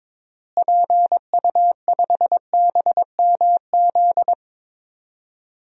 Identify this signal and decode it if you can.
Morse code, heard as a keyed tone, about 22 wpm, 700 Hz. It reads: PU5BMZ
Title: DXCC-100 Call Signs - 6 Characters - Repeated 3x - 22wpm